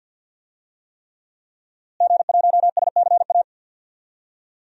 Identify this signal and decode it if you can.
Morse code, heard as a keyed tone, about 53 wpm, 700 Hz. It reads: G1SCA
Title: DXCC-100 Call Signs - 5 Characters - Repeated 3x - 50wpm